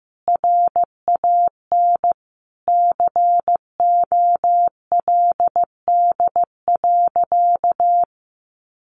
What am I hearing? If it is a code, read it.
Morse code, heard as a keyed tone, about 15 wpm, 700 Hz. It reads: RAN COLD.
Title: Sentences from Top 1100 Words 15wpm